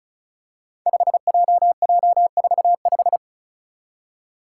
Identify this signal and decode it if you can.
Morse code, heard as a keyed tone, about 34 wpm, 700 Hz. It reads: 5JJ45
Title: Five Challenging Letter-Number Combinations 35wpm